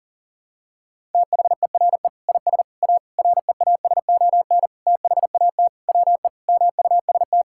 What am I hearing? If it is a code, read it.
Morse code, heard as a keyed tone, about 40 wpm, 700 Hz. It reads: THERE IS A REASON THAT WE MUST
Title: Sentences from Top 1000 Words - Review Part 5 - 40wpm